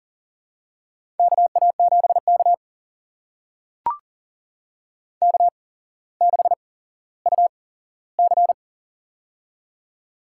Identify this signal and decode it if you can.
Morse code, heard as a keyed tone, about 43 wpm, spaced as for 17 wpm, 700 Hz. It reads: KA7K K6UC
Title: USA Call Signs - 4 Characters - 8x Character Spacing - 40wpm